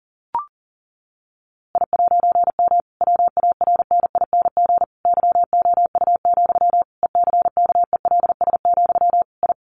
Morse code, heard as a keyed tone, about 37 wpm, 700 Hz. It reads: I'M WARNING YOU, ECKELS, I
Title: A Sound Of Thunder 40wpm